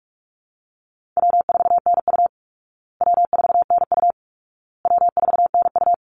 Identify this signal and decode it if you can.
Morse code, heard as a keyed tone, about 48 wpm, 700 Hz. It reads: W4NU W4NU W4NU
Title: Easy Call Signs Repeated 3x - 45wpm